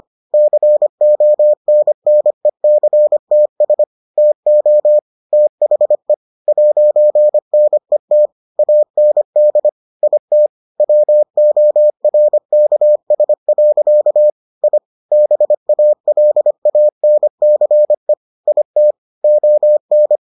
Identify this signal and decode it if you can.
Morse code, heard as a keyed tone, about 25 wpm, 600 Hz. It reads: CONNECTS TO THE 'NET AND IT WORKS. I BALANCE IT ON